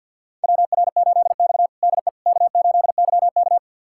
Morse code, heard as a keyed tone, about 51 wpm, 700 Hz. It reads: WR8X DE K8YK
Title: USA Callsign DE Callsign - 4 Characters - 50wpm